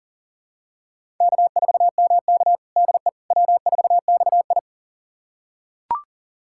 Morse code, heard as a keyed tone, about 37 wpm, 700 Hz. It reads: K4MK DE W4XI
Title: USA Callsign DE Callsign - 4 Characters - 40wpm